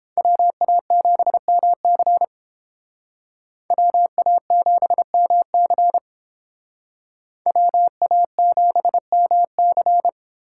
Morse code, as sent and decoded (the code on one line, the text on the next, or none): .-- .- --... -- -.-. / .-- .- --... -- -.-. / .-- .- --... -- -.-.
WA7MC WA7MC WA7MC